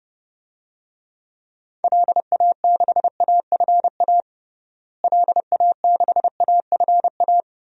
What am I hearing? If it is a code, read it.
Morse code, heard as a keyed tone, about 32 wpm, 700 Hz. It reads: LA6AFA LA6AFA
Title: DXCC-100 Call Signs - 6 Characters - Repeated 2x - 30wpm